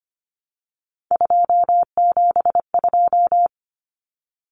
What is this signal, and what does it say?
Morse code, heard as a keyed tone, about 25 wpm, 700 Hz. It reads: 272
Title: Numbers - 3-Digit 25wpm